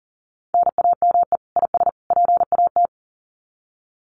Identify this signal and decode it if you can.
Morse code, heard as a keyed tone, about 40 wpm, 700 Hz. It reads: NAME IS PAT
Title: Common QSO Phrases 40wpm